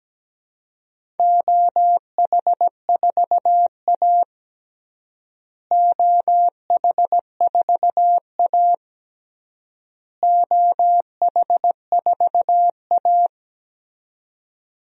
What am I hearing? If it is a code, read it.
Morse code, heard as a keyed tone, about 17 wpm, 700 Hz. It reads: OH4A OH4A OH4A
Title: DXCC-100 Call Signs - 4 Characters - Repeated 3x - 17wpm